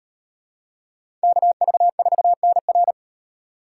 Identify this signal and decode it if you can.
Morse code, heard as a keyed tone, about 38 wpm, 700 Hz. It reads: KV4NR